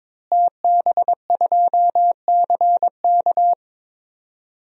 Morse code, heard as a keyed tone, about 22 wpm, 700 Hz. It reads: TB2CK
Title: DXCC-100 Call Signs - 5 Characters - Repeated 2x - 22wpm